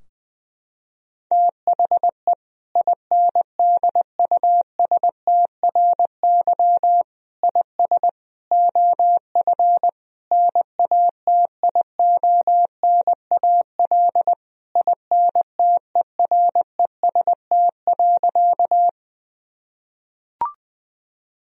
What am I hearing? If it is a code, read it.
Morse code, heard as a keyed tone, about 20 wpm, 700 Hz. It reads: THE INDUSTRY IS OF NATIONAL INTEREST.
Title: Sentences from Top 1000 Words - Review Part 4 - 20wpm